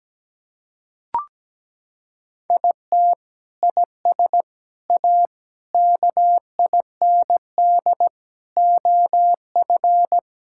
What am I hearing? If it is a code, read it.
Morse code, heard as a keyed tone, about 17 wpm, 700 Hz. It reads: IT IS A KIND OF
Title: Sentences from Top 1000 Words - Review Part 5 - 17wpm